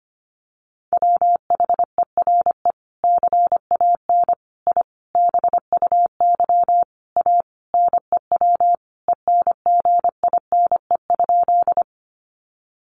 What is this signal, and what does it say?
Morse code, heard as a keyed tone, about 25 wpm, 700 Hz. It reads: WHERE CAN I BUY A NEW ENGINE?
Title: Sentences from Top 1500 Words 25wpm